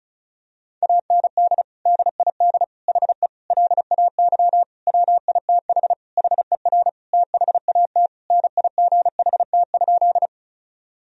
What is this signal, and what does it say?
Morse code, heard as a keyed tone, about 36 wpm, 700 Hz. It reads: AND DID HE LAY WITH HER THAT NIGHT?
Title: Sentences from Top 1000 Words - Review Part 2 - 35wpm